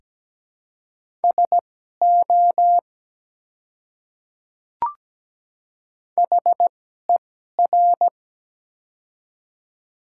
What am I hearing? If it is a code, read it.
Morse code, heard as a keyed tone, about 17 wpm, spaced as for 12 wpm, 700 Hz. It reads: SO HER
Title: 2x Character Spacing - Top 100 Words - 17wpm